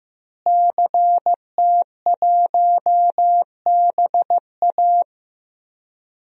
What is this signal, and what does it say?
Morse code, heard as a keyed tone, about 15 wpm, 700 Hz. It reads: CT1BA
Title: DXCC-100 Call Signs - 5 Characters - Repeated 3x - 15wpm